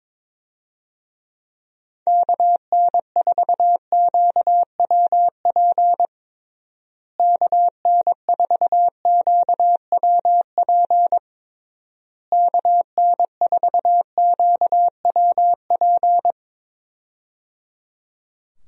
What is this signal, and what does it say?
Morse code, heard as a keyed tone, about 21 wpm, 700 Hz. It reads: KN4QWP KN4QWP KN4QWP
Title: USA Call Signs - 6 Characters - Repeated 3x - 22wpm